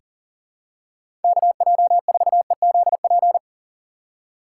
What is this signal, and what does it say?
Morse code, heard as a keyed tone, about 40 wpm, 700 Hz. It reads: KJ4EZP
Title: USA Call Signs - 6 Characters - 3x Character Spacing - 40wpm